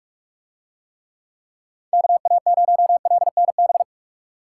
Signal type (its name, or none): Morse code